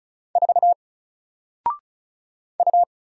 Morse code, heard as a keyed tone, about 35 wpm, 700 Hz. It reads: 4 U
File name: Single-Letter-Number-35wpm